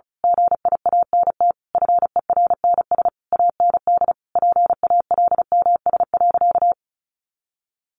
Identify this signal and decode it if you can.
Morse code, heard as a keyed tone, about 34 wpm, 700 Hz. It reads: GIANT FERNS AND PALMS.
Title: A Sound Of Thunder 35wpm